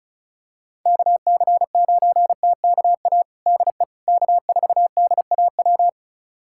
Morse code, heard as a keyed tone, about 34 wpm, 700 Hz. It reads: KC9TKA DE K4DAW
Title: USA Callsign DE Callsign 35wpm